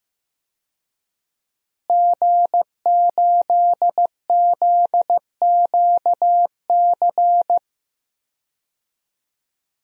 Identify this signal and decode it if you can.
Morse code, heard as a keyed tone, about 15 wpm, 700 Hz. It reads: G8ZQC
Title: DXCC-100 Call Signs - 5 Characters - Repeated 3x - 15wpm